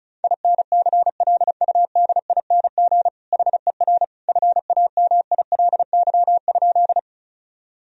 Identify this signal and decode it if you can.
Morse code, heard as a keyed tone, about 34 wpm, 700 Hz. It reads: INCLUDING HER FAMILY?
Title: Sentences from Top 1000 Words - Review Part 4 - 35wpm